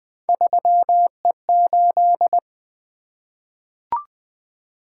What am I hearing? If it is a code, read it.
Morse code, heard as a keyed tone, about 20 wpm, 700 Hz. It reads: 3E8
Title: Three Letter-Number Combinations - 20wpm